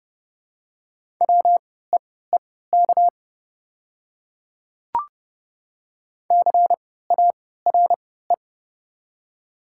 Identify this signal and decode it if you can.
Morse code, heard as a keyed tone, about 30 wpm, spaced as for 17 wpm, 700 Hz. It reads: WEEK CARE
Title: 3x Character Spacing - Top 500 Words - 30wpm